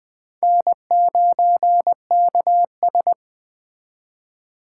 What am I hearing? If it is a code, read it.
Morse code, heard as a keyed tone, about 21 wpm, 700 Hz. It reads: N9KS
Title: Speed Racing - USA 1x2 Call Signs 17wpm